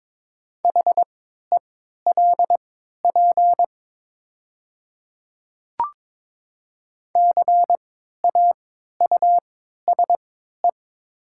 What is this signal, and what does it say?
Morse code, heard as a keyed tone, about 22 wpm, spaced as for 13 wpm, 700 Hz. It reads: HELP CAUSE
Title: 3x Character Spacing - Top 200 Words - 22wpm